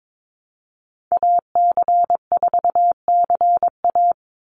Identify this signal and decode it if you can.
Morse code, heard as a keyed tone, about 22 wpm, 700 Hz. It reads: AC4CA
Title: Call Signs Repeated 3x - 22wpm